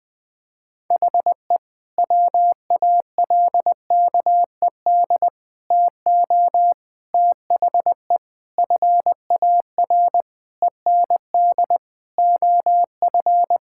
Morse code, heard as a keyed tone, about 20 wpm, 700 Hz. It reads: HE WALKED TO THE FAR END OF
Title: Sentences with 5-Word Phrases - 20wpm